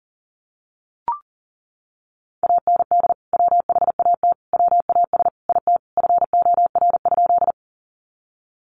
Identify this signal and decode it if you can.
Morse code, heard as a keyed tone, about 40 wpm, 700 Hz. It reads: AND WHAT WAS IT FOR?
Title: Sentences from Top 100 Words 40wpm